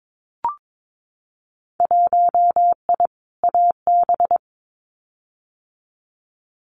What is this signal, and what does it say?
Morse code, heard as a keyed tone, about 22 wpm, 700 Hz. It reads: 1I AB
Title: Winter Field Day Exchanges Encore 22wpm